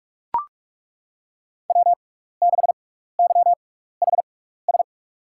Morse code, heard as a keyed tone, about 41 wpm, spaced as for 15 wpm, 700 Hz. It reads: W6YHS
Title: USA Call Signs - 5 Characters - 6x Character Spacing - 45wpm